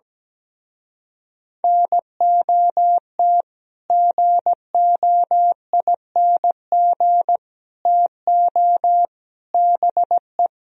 Morse code, heard as a keyed tone, about 17 wpm, 700 Hz. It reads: NOT GOING TO BE
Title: Speed-Racing - Sets of 4 Words 17wpm